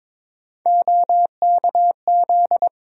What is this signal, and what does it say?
Morse code, heard as a keyed tone, about 21 wpm, 700 Hz. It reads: OKZ
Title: USA Call Sign Postfixes - 3 Letters 22wpm